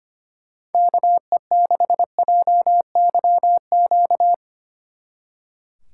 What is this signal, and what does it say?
Morse code, heard as a keyed tone, about 31 wpm, 700 Hz. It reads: KE6JYQ
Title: Speed Racing - USA Call Signs 25wpm